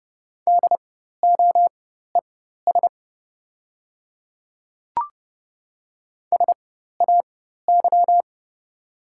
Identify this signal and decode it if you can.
Morse code, heard as a keyed tone, about 30 wpm, spaced as for 14 wpm, 690 Hz. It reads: DOES SAY
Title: 4x Character Spacing - Top 200 Words - 30wpm